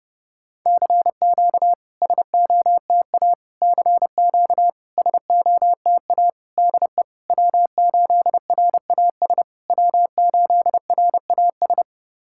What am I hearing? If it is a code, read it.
Morse code, heard as a keyed tone, about 30 wpm, 700 Hz. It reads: CQ SOTA CQ SOTA DE W8RAS W8RAS
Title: SOTA QSO Practice 30wpm